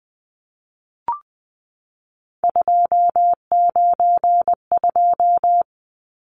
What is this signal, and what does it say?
Morse code, heard as a keyed tone, about 20 wpm, 700 Hz. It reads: 292
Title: Numbers - 3-Digit 20wpm